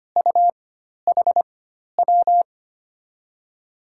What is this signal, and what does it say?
Morse code, heard as a keyed tone, about 25 wpm, spaced as for 12 wpm, 700 Hz. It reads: UHW
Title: USA Call Signs - 6 Characters - 4x Character Spacing - 25wpm